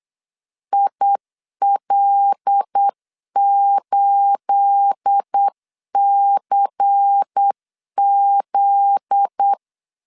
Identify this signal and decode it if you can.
Morse code, heard as a keyed tone, about 8 wpm, 800 Hz. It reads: IL8CZ
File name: LEZIONI / 9 28 8wpm 3512